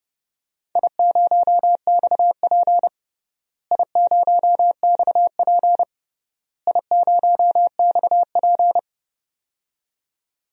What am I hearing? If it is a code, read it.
Morse code, heard as a keyed tone, about 30 wpm, 700 Hz. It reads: I0XP I0XP I0XP
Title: DXCC-100 Call Signs - 4 Characters - Repeated 3x - 30wpm